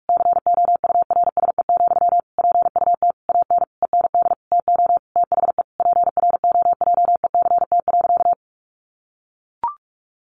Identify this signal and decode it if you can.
Morse code, heard as a keyed tone, about 44 wpm, 700 Hz. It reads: COURSE, PUT AN END TO THE PROJECT.
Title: Sentences from Top 1400 Words 45wpm